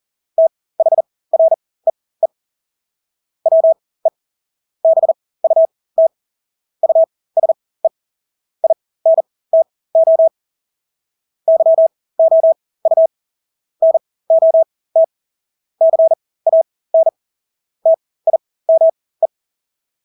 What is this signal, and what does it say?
Morse code, heard as a keyed tone, about 40 wpm, spaced as for 20 wpm, 650 Hz. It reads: THREE WE BUT USE INTO YOU NOT CAN TIME
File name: Engelske ord 40-20 wpm